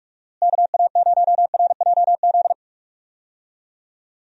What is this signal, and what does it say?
Morse code, heard as a keyed tone, about 43 wpm, 700 Hz. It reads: KA0RJZ
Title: USA Call Signs - 6 Characters - Repeated 3x - 45wpm